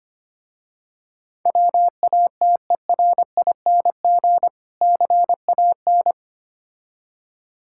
Morse code, heard as a keyed tone, about 25 wpm, 690 Hz. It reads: WATERING CAN
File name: practice / 25wpm / Difficult Words with Hints - 25wpm